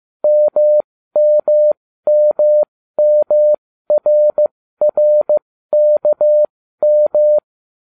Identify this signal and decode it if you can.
Morse code, heard as a keyed tone, about 15 wpm, spaced as for 13 wpm, 600 Hz. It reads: MMMMRRKM